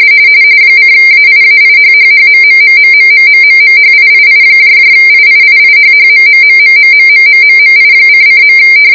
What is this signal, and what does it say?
RTTY, about 45.45 baud. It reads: :AIAIHHHMMMMMMMMMHIIIIIIIHHMMMMMMMMMMI::.